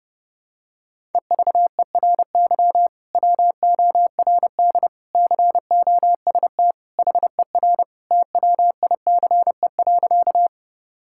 Morse code, heard as a keyed tone, about 30 wpm, 700 Hz. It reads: EVERY WORD COST HER TWICE.
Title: Sentences from Top 1600 Words 30wpm